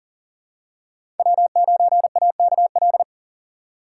Morse code, heard as a keyed tone, about 40 wpm, 700 Hz. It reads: W9AKL
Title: USA Call Signs - 5 Characters - 4x Character Spacing - 40wpm